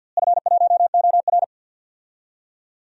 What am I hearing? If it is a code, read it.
Morse code, heard as a keyed tone, about 53 wpm, 700 Hz. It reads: F1OR
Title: DXCC-100 Call Signs - 4 Characters - Repeated 2x - 50wpm